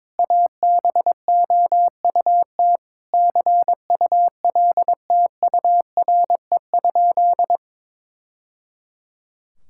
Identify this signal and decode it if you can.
Morse code, heard as a keyed tone, about 22 wpm, 700 Hz. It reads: ABOUT CULTURE?
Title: Sentences from Top 1000 Words - Review Part 6 - 22wpm